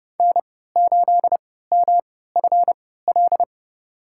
Morse code, heard as a keyed tone, about 30 wpm, spaced as for 17 wpm, 700 Hz. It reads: N8MFL
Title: USA Call Signs - 3x Character Spacing - 30wpm